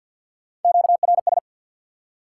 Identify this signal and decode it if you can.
Morse code, heard as a keyed tone, about 53 wpm, 700 Hz. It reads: QRS
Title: Top 90 QSO Elements 50wpm